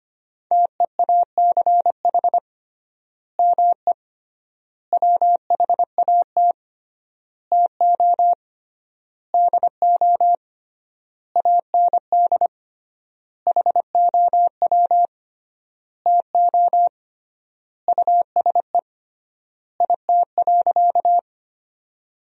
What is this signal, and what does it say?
Morse code, heard as a keyed tone, about 25 wpm, 700 Hz. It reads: TEACH ME WHAT TO DO AND HOW TO USE IT.
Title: 3x Word-Spacing - Sentences from Top 500 Words - 25wpm